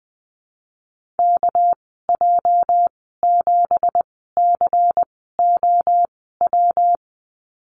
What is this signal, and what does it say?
Morse code, heard as a keyed tone, about 20 wpm, spaced as for 15 wpm, 700 Hz. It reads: KJ7COW
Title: 213 Lesson 34g - Number 8 - Call Signs 20wpm